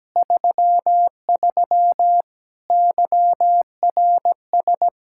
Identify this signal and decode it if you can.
Morse code, heard as a keyed tone, about 17 wpm, 700 Hz. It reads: 33 YRS
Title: Practice QSO - Exchange 3 - More Info - 17wpm